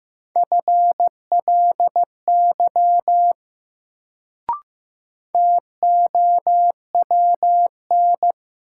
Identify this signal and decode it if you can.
Morse code, heard as a keyed tone, about 15 wpm, 700 Hz. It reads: FLY TOWN
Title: Sentences from Top 500 Words - New Word Drill - 15wpm